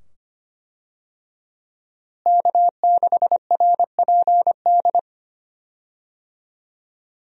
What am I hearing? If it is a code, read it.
Morse code, heard as a keyed tone, about 25 wpm, 700 Hz. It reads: K6RPD
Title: USA Call Signs - 5 Characters - Repeated 2x - 25wpm